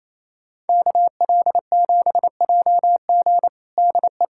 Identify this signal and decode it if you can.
Morse code, heard as a keyed tone, about 28 wpm, 700 Hz. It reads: KL7JG DE
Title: USA Callsign DE Callsign - 5 Characters - 28wpm